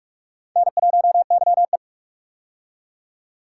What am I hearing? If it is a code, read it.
Morse code, heard as a keyed tone, about 46 wpm, 700 Hz. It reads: N1YE
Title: K1USN SST QSO Practice - 45wpm